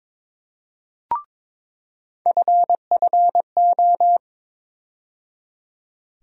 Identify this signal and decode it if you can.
Morse code, heard as a keyed tone, about 22 wpm, 700 Hz. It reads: FFO
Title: USA Call Sign Postfixes - 3 Letters 22wpm